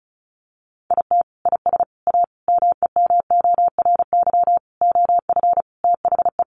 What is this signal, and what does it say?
Morse code, heard as a keyed tone, about 34 wpm, 700 Hz. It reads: IT IS A MEMORY OF THE
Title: Sentences from Top 1300 Words 35wpm